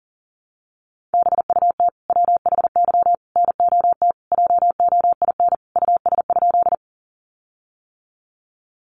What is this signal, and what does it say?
Morse code, heard as a keyed tone, about 40 wpm, 700 Hz. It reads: BUT WHY NOT JOIN US?
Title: Sentences from Top 1400 Words 40wpm